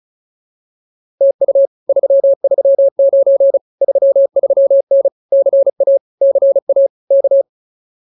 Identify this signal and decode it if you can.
Morse code, heard as a keyed tone, about 34 wpm, 550 Hz. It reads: TU 339 33N CA CA K
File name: POTA106 35